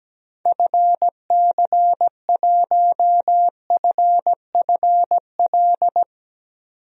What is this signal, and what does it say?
Morse code, heard as a keyed tone, about 17 wpm, 700 Hz. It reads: FC1FFL
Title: DXCC-100 Call Signs - 6 Characters - 17wpm